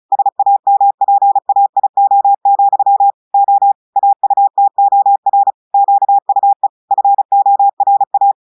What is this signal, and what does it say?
Morse code, heard as a keyed tone, about 36 wpm, 800 Hz. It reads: SAMPAIO, O AUTOR QUE FORA
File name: audios / portugais / portugais texte en clair 1 vitesse 35 mots minutes